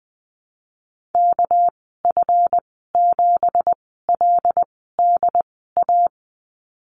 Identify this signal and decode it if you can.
Morse code, heard as a keyed tone, about 20 wpm, 700 Hz. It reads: K F 7 L D A
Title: 207 Lesson 33g - Letter j - Call Signs 20wpm